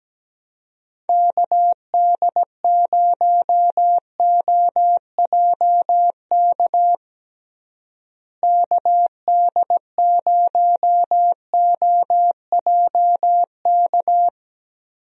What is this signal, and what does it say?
Morse code, heard as a keyed tone, about 17 wpm, 700 Hz. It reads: KD0OJK KD0OJK
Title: USA Call Signs - 6 Characters - Repeated 3x - 17wpm